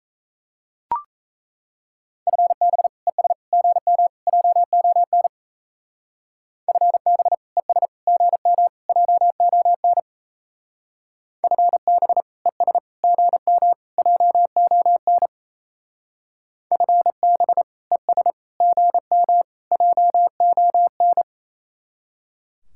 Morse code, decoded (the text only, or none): FB ES GM JON FB ES GM JON FB ES GM JON FB ES GM JON